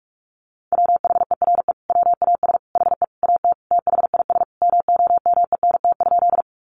Morse code, heard as a keyed tone, about 44 wpm, 700 Hz. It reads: WHERE WAS HE AT THIS MOMENT?
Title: Sentences from Top 1000 Words 45wpm